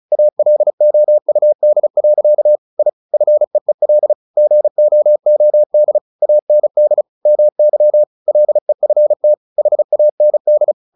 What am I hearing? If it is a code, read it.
Morse code, heard as a keyed tone, about 34 wpm, 600 Hz. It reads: ALOUD. I FEEL GOOD AND MY LEFT HAND
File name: the old man and the sea 2 35wpm